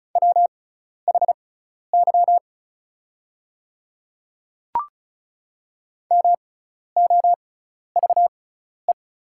Morse code, heard as a keyed tone, about 36 wpm, spaced as for 12 wpm, 700 Hz. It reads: WHY MOVE
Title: 6x Character Spacing - Top 200 Words - 35wpm